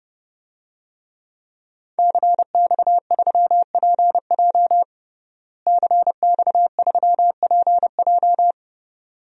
Morse code, heard as a keyed tone, about 30 wpm, 700 Hz. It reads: CX3PJ CX3PJ
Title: DXCC-100 Call Signs - 5 Characters - Repeated 3x - 30wpm